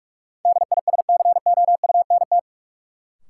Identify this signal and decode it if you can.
Morse code, heard as a keyed tone, about 42 wpm, 700 Hz. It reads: DISCOUNT
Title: Sentences from Top 1200 Words - New Word Drill - 45wpm